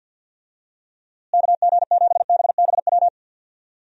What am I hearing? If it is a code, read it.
Morse code, heard as a keyed tone, about 50 wpm, 700 Hz. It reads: KG7BBW
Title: USA Call Signs - 6 Characters - Repeated 3x - 50wpm